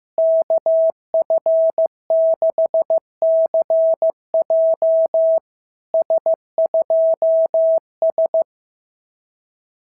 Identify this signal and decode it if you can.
Morse code, heard as a keyed tone, about 15 wpm, 660 Hz. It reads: KF6CJ S2S
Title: SOTA QSO Practice 15wpm